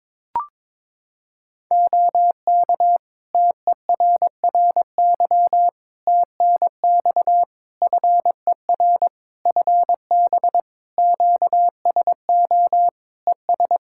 Morse code, heard as a keyed tone, about 22 wpm, 700 Hz. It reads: OK TERRY TNX FER FB QSO ES
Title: Practice QSO - Exchange 4 - Ending - 22wpm